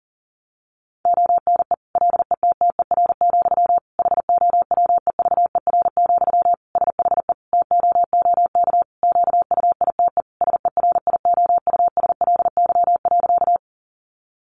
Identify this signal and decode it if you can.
Morse code, heard as a keyed tone, about 40 wpm, 700 Hz. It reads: ONE LETTER, HOWEVER, SHE TOOK QUITE SERIOUSLY.
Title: Sentences from Top 1200 Words 40wpm